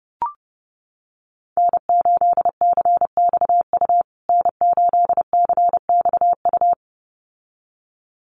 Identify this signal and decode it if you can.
Morse code, heard as a keyed tone, about 30 wpm, 700 Hz. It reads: N8CXU N8CXU
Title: USA Call Signs Repeated 2x 30wpm